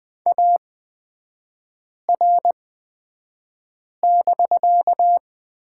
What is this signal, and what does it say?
Morse code, heard as a keyed tone, about 20 wpm, spaced as for 5 wpm, 700 Hz. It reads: AR<BK>
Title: 250 Lesson 40e - BK - Three Characters 20wpm